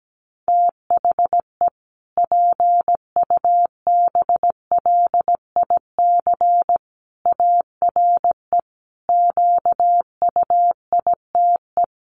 Morse code, heard as a keyed tone, about 17 wpm, 700 Hz. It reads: THE PUBLIC ARE QUITE
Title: Sentences from Top 1300 Words 17wpm